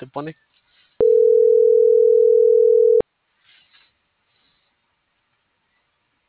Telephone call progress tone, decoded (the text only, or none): ringback tone